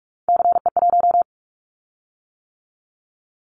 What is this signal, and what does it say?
Morse code, heard as a keyed tone, about 46 wpm, 700 Hz. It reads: CE1
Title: Callsign Prefixes for Top 100 DXCC - 45wpm